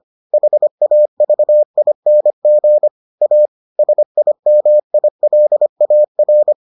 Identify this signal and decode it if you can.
Morse code, heard as a keyed tone, about 25 wpm, 600 Hz. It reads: HAVING A SIMILAR